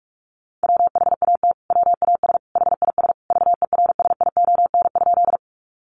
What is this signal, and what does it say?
Morse code, heard as a keyed tone, about 44 wpm, 700 Hz. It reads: WHAT WAS HIS VERSION?
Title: Sentences from Top 1400 Words 45wpm